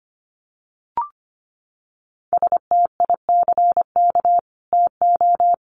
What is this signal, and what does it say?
Morse code, heard as a keyed tone, about 25 wpm, 700 Hz. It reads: STICK TO
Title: Sentences from Top 1200 Words 25wpm